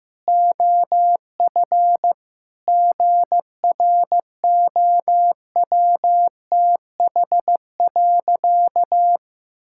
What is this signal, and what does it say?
Morse code, heard as a keyed tone, about 15 wpm, 700 Hz. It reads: OF GROWTH.
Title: Sentences from Top 1000 Words - Review Part 5 - 15wpm